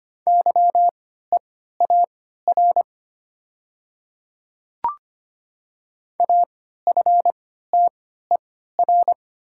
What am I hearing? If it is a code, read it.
Morse code, heard as a keyed tone, about 25 wpm, spaced as for 14 wpm, 700 Hz. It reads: YEAR AFTER
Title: 3x Character Spacing - Top 200 Words - 25wpm